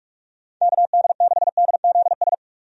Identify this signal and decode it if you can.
Morse code, heard as a keyed tone, about 46 wpm, 700 Hz. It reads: KD6DZS
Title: USA Call Signs - 6 Characters - 45wpm